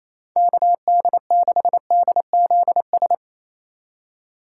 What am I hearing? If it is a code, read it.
Morse code, heard as a keyed tone, about 28 wpm, 700 Hz. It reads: KD6DZS
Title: USA Call Signs - 3x Character Spacing - 28wpm